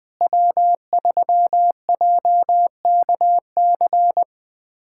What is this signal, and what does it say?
Morse code, heard as a keyed tone, about 21 wpm, 700 Hz. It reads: W3JKC
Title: USA Callsign DE Callsign 20wpm